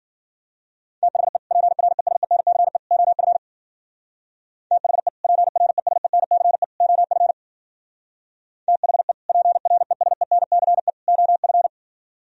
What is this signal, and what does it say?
Morse code, heard as a keyed tone, about 53 wpm, 700 Hz. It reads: THE PRESENCE OF THE PRESENCE OF THE PRESENCE OF